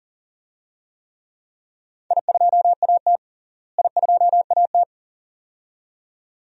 Morse code, heard as a keyed tone, about 40 wpm, 700 Hz. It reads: I2AT I2AT
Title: DXCC-100 Call Signs - 4 Characters - Repeated 2x - 40wpm